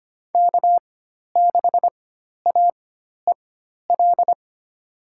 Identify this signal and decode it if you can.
Morse code, heard as a keyed tone, about 25 wpm, spaced as for 12 wpm, 700 Hz. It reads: K6AEL
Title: USA Call Signs - 4x Character Spacing - 25wpm